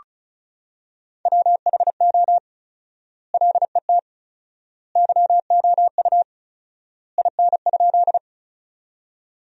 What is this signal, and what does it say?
Morse code, heard as a keyed tone, about 34 wpm, 700 Hz. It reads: WHO LET YOU IN?
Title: 4x Word-Spacing - Sentences from Top 300 Words - 35wpm